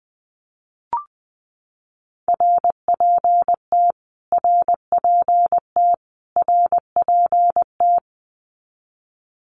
Morse code, heard as a keyed tone, about 20 wpm, 700 Hz. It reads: RPT RPT RPT
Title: Abbreviations Repeated 3x - 20wpm